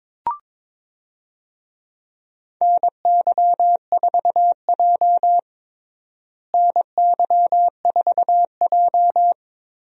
Morse code, heard as a keyed tone, about 22 wpm, 700 Hz. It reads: NY4J NY4J
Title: USA Call Signs - 4 Characters - Repeated 3x - 22wpm